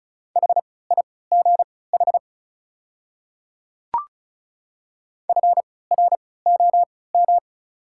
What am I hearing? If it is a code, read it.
Morse code, heard as a keyed tone, about 34 wpm, spaced as for 20 wpm, 690 Hz. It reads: HIGH FROM
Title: 3x Character Spacing - Top 500 Words - 35wpm